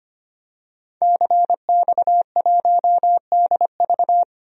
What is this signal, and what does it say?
Morse code, heard as a keyed tone, about 25 wpm, 700 Hz. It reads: CX1DV
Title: DXCC-100 Call Signs - 5 Characters - 25wpm